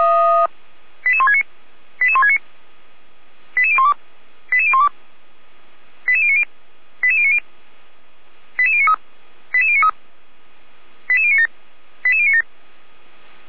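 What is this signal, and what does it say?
Five-tone selective call (ZVEI), heard as quick runs of five tones, tones of about 70 ms. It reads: 88169 88169 88012 88012 88090 88090 88093 88093 88097 88097
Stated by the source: ZVEI five-tone selective call received on SDR, from feuerwehr-orlamuende.org fire service audio server.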